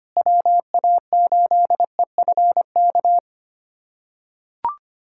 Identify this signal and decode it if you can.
Morse code, heard as a keyed tone, about 25 wpm, 700 Hz. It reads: WA8EFK
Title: USA Callsign DE Callsign 25wpm